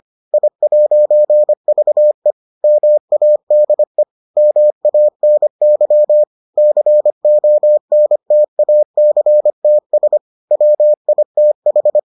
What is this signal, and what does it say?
Morse code, heard as a keyed tone, about 25 wpm, 600 Hz. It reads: I'VE MADE MANY CONTACTS WITH